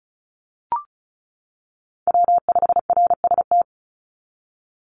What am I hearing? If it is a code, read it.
Morse code, heard as a keyed tone, about 34 wpm, 700 Hz. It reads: W5RST
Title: NA SOTA Chaser Callsigns 35wpm